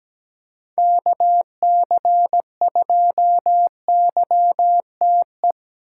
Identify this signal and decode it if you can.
Morse code, heard as a keyed tone, about 17 wpm, 700 Hz. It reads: KC2YTE
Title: USA Call Signs - 6 Characters - 2x Character Spacing - 17wpm